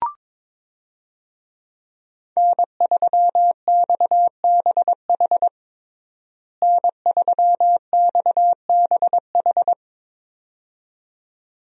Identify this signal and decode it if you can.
Morse code, heard as a keyed tone, about 22 wpm, 700 Hz. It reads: N3XBH N3XBH
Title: USA Call Signs - 5 Characters - Repeated 2x - 22wpm